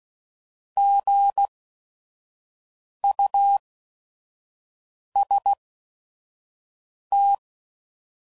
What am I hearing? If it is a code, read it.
Morse code, heard as a keyed tone, about 16 wpm, spaced as for 5 wpm, 800 Hz. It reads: GUST
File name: Prefixes / By First Letter / Gs